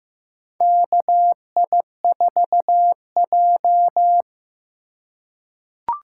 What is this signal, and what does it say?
Morse code, heard as a keyed tone, about 16 wpm, 700 Hz. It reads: KI4J
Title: USA Call Signs - 4 Characters - 2x Character Spacing - 15wpm